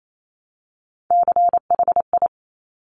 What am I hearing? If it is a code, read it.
Morse code, heard as a keyed tone, about 28 wpm, 700 Hz. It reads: CHI
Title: Top Three Letter Combinations in Words 28wpm